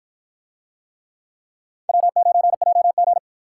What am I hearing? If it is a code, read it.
Morse code, heard as a keyed tone, about 53 wpm, 700 Hz. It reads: W9JG